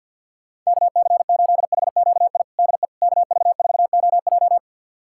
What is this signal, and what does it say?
Morse code, heard as a keyed tone, about 51 wpm, 700 Hz. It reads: KC8HQI DE KV4OJ
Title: USA Callsign DE Callsign 50wpm